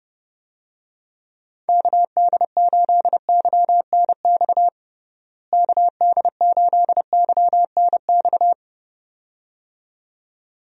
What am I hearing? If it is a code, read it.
Morse code, heard as a keyed tone, about 31 wpm, 700 Hz. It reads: KD8YNX KD8YNX
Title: USA Call Signs - 6 Characters - Repeated 2x - 30wpm